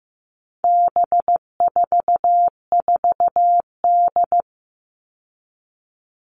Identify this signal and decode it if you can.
Morse code, heard as a keyed tone, about 15 wpm, 700 Hz. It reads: B44D
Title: All Cut Numbers - 4-Digit 15wpm